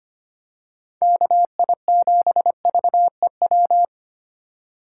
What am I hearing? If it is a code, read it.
Morse code, heard as a keyed tone, about 25 wpm, 690 Hz. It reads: KI7VEW
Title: USA Call Signs - 4x Character Spacing - 25wpm